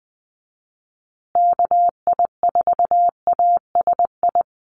Morse code, heard as a keyed tone, about 20 wpm, 700 Hz. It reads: KI4ASI
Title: USA Call Signs Repeated 3x 20wpm